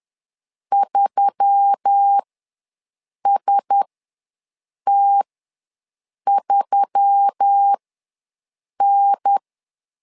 Morse code, heard as a keyed tone, about 10 wpm, spaced as for 6 wpm, 800 Hz. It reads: 3ST3N
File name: LEZIONI / 4 Q1S3 ws 10wpm 2437